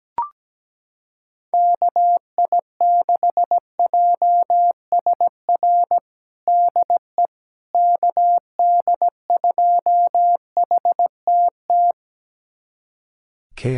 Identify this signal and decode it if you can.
Morse code, heard as a keyed tone, about 17 wpm, 700 Hz. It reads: KI6JSR DE KD2HTT
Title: USA Callsign DE Callsign - 6 Characters - 17wpm